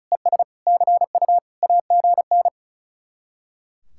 Morse code, heard as a keyed tone, about 45 wpm, 700 Hz. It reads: ES CU AGN